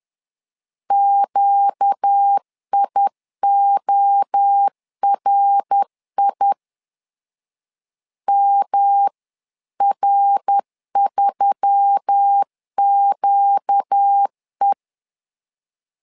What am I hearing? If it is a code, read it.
Morse code, heard as a keyed tone, about 10 wpm, 800 Hz. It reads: QIORI MR3QE